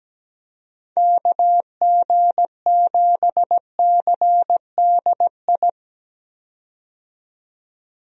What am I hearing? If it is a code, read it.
Morse code, heard as a keyed tone, about 17 wpm, 700 Hz. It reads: KG7CDI
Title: USA Call Signs - 6 Characters - 17wpm